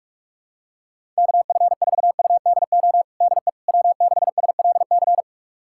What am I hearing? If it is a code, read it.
Morse code, heard as a keyed tone, about 46 wpm, 700 Hz. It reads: KF4UDO DE W6SLC
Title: USA Callsign DE Callsign 45wpm